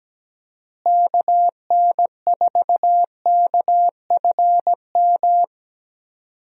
Morse code, heard as a keyed tone, about 17 wpm, 700 Hz. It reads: KN4KFM
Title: USA Call Signs - 6 Characters - Repeated 3x - 17wpm